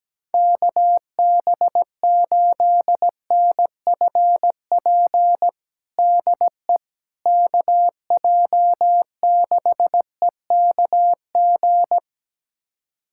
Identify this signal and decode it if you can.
Morse code, heard as a keyed tone, about 17 wpm, 700 Hz. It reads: KB8NFP DE KJ6EKG
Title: USA Callsign DE Callsign - 6 Characters - 17wpm